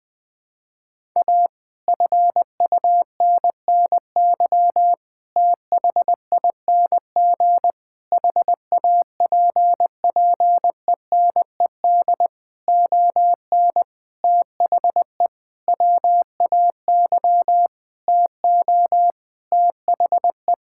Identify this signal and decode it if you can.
Morse code, heard as a keyed tone, about 20 wpm, 700 Hz. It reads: A FUNNY THING HAPPENED ON THE WAY TO THE
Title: Sentences with 8-Word Phrases - 20wpm